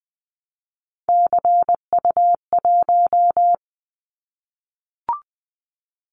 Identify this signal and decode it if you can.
Morse code, heard as a keyed tone, about 20 wpm, 700 Hz. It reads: CU1
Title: Callsign Prefixes for Top 100 DXCC - 20wpm